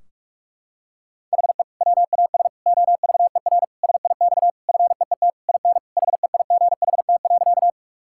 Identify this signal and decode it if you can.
Morse code, heard as a keyed tone, about 46 wpm, 700 Hz. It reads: HE WAS OVER SIX FEET IN HEIGHT.
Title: Sentences from Top 1600 Words 45wpm